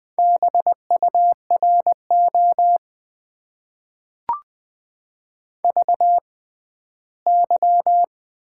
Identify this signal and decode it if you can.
Morse code, heard as a keyed tone, about 20 wpm, spaced as for 13 wpm, 700 Hz. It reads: BURO VY
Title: Top 90 QSO Elements - 6x Character Spacing - 20wpm